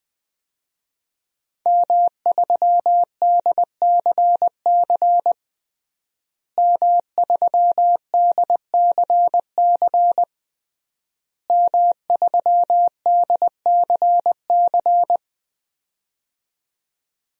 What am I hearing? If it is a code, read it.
Morse code, heard as a keyed tone, about 20 wpm, 700 Hz. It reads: M3DCC M3DCC M3DCC
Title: DXCC-100 Call Signs - 5 Characters - Repeated 3x - 20wpm